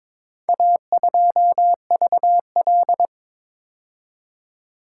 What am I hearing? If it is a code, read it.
Morse code, heard as a keyed tone, about 22 wpm, 700 Hz. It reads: A2VL
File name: practice / 22wpm / Four Letter-Number Combinations - 22wpm